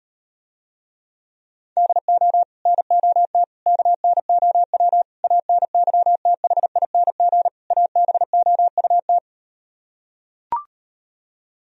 Morse code, heard as a keyed tone, about 38 wpm, 700 Hz. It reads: DO NOT KNOW ANYTHING ABOUT